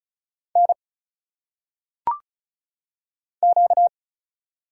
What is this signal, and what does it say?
Morse code, heard as a keyed tone, about 35 wpm, 700 Hz. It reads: N Q
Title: Single Letters - Phonetic Pronunciation 35wpm